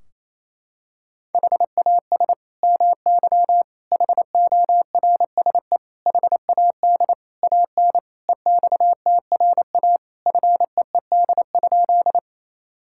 Morse code, heard as a keyed tone, about 28 wpm, 700 Hz. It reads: HAS MY HORSE HAD AN EXTRA FEED?
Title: Sentences from Top 1600 Words 28wpm